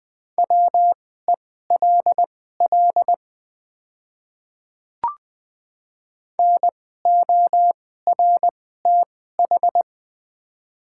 Morse code, heard as a keyed tone, about 20 wpm, spaced as for 15 wpm, 700 Hz. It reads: WELL NORTH
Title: 2x Character Spacing - Top 300 Words - 20wpm